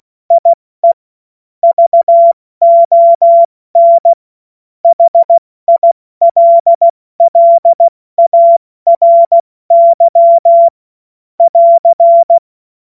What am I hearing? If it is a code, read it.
Morse code, heard as a keyed tone, about 16 wpm, 680 Hz. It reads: IE VON HILLARY <AR>